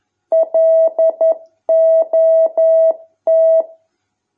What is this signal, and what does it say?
Morse code, heard as a keyed tone, about 11 wpm, 650 Hz. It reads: LOT